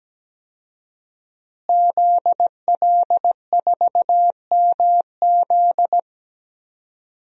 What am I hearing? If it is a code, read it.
Morse code, heard as a keyed tone, about 17 wpm, 700 Hz. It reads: ZL4MZ
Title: DXCC-100 Call Signs - 5 Characters - Repeated 2x - 17wpm